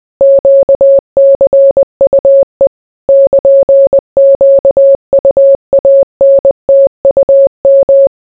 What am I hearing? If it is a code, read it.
Morse code, heard as a keyed tone, about 20 wpm, 550 Hz. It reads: QCUE <KN>QUANTUM